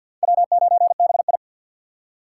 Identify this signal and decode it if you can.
Morse code, heard as a keyed tone, about 46 wpm, 700 Hz. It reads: W9BI
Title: USA Call Signs - 4 Characters - 2x Character Spacing - 50wpm